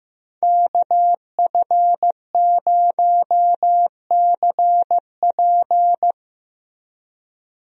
Morse code, heard as a keyed tone, about 15 wpm, 700 Hz. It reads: KF0CP
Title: USA Call Signs - 5 Characters - Repeated 3x - 15wpm